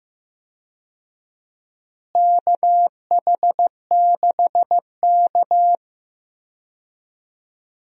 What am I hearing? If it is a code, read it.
Morse code, heard as a keyed tone, about 15 wpm, 700 Hz. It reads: KH6K
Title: DXCC-100 Call Signs - 4 Characters - Repeated 2x - 15wpm